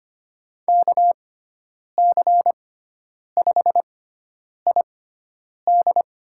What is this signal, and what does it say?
Morse code, heard as a keyed tone, about 25 wpm, spaced as for 9 wpm, 700 Hz. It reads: KC5ID
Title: USA Call Signs - 6 Characters - 6x Character Spacing - 25wpm